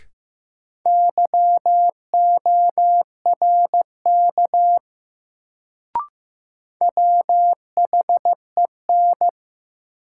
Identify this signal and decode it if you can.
Morse code, heard as a keyed tone, about 15 wpm, 700 Hz. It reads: YORK WHEN
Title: Sets of 2 Words - New Word Drill - 15wpm